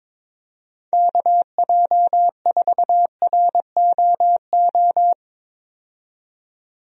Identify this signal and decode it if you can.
Morse code, heard as a keyed tone, about 22 wpm, 700 Hz. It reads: KJ4ROO
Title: USA Call Signs - 6 Characters - 22wpm